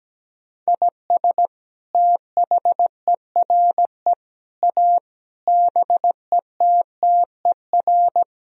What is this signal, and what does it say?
Morse code, heard as a keyed tone, about 17 wpm, 700 Hz. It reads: IS THERE A BETTER
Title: Sentences from Top 1000 Words - Review Part 4 - 17wpm